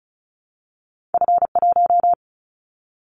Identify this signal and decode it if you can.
Morse code, heard as a keyed tone, about 36 wpm, 700 Hz. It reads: F1
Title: Callsign Prefixes for Top 50 DXCC - 35wpm